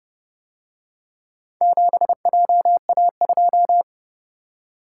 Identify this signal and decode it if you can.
Morse code, heard as a keyed tone, about 30 wpm, 700 Hz. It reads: 7JA2